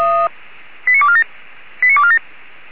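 Five-tone selective call (ZVEI), heard as quick runs of five tones, tones of about 70 ms. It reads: 89247 89247
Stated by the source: ZVEI five-tone selective call received via SDR.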